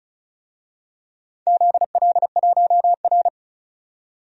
Morse code, heard as a keyed tone, about 36 wpm, 690 Hz. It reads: ZL1R
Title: DXCC-100 Call Signs - 4 Characters - 35wpm